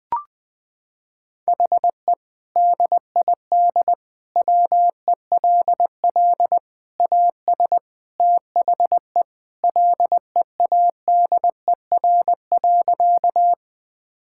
Morse code, heard as a keyed tone, about 20 wpm, 700 Hz. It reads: HE DID WELL AS THE LEADER.
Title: Sentences with 3-Word Phrases - 20wpm